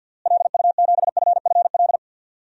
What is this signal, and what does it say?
Morse code, heard as a keyed tone, about 52 wpm, 700 Hz. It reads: LU7FFL
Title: DXCC-100 Call Signs - 6 Characters - Repeated 3x - 50wpm